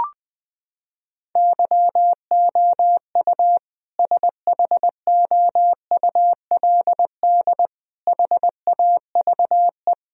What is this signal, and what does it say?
Morse code, heard as a keyed tone, about 21 wpm, 700 Hz. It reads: YOU SHOULD HAVE
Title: Sentences from Top 1000 Words - Review Part 4 - 20wpm